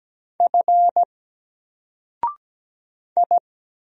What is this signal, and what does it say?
Morse code, heard as a keyed tone, about 17 wpm, 700 Hz. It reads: F I
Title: Single Character (Letter:Number:Punctuation:Prosign) Alt 17wpm